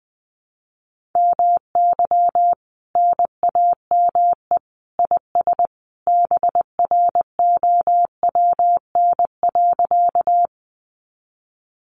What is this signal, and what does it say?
Morse code, heard as a keyed tone, about 20 wpm, 700 Hz. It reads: MY NAME IS BROWN.
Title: Sentences from Top 1200 Words 20wpm